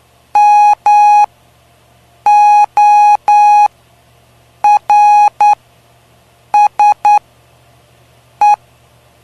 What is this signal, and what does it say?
Morse code, heard as a keyed tone, about 9 wpm, spaced as for 6 wpm, 830 Hz. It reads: MORSE
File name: cw / py2lcd / 05-PY2LCD-Li��o 5